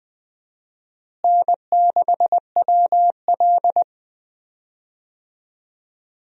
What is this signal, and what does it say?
Morse code, heard as a keyed tone, about 20 wpm, 690 Hz. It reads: N6WL
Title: USA Call Signs - 4 Characters - Repeated 3x - 20wpm